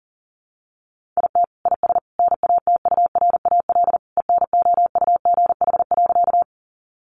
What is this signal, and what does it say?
Morse code, heard as a keyed tone, about 40 wpm, 700 Hz. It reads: IT IS NATURAL ENOUGH.
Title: Sentences from Top 700 Words 40wpm